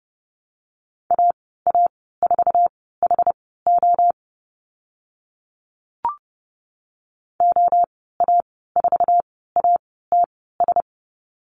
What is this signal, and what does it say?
Morse code, heard as a keyed tone, about 30 wpm, spaced as for 17 wpm, 700 Hz. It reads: AA4HO OA4ATS
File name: Lesson 14g - Number 2 - Call Signs 30wpm